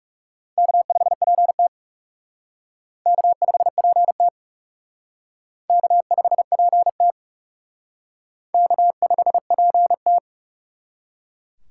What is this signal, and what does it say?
Morse code, heard as a keyed tone, about 36 wpm, 700 Hz. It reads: K5PT K5PT K5PT K5PT
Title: Speed Racing - USA 1x2 Call Signs 30wpm